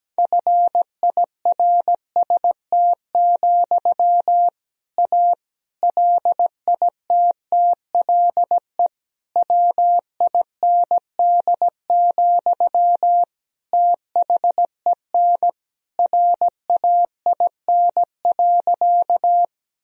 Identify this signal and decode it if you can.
Morse code, heard as a keyed tone, about 17 wpm, 700 Hz. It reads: FIRST, A LITTLE WIND, THEN RAIN.
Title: Sentences from Top 1000 Words - Review Part 2 - 17wpm